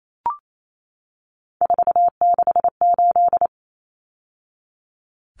Morse code, heard as a keyed tone, about 28 wpm, 700 Hz. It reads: 468
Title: Numbers - 3-Digit 28wpm